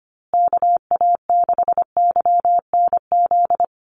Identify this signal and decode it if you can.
Morse code, heard as a keyed tone, about 25 wpm, 700 Hz. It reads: KA6YNZ
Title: USA Call Signs Repeated 2x 25wpm